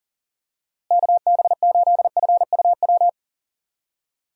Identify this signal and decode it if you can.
Morse code, heard as a keyed tone, about 40 wpm, 700 Hz. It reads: KB8FUW
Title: USA Call Signs - 6 Characters - Repeated 3x - 40wpm